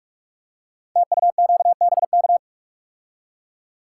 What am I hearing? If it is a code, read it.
Morse code, heard as a keyed tone, about 44 wpm, 700 Hz. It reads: TUQBK
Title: Code Groups - 5 Letters - 45wpm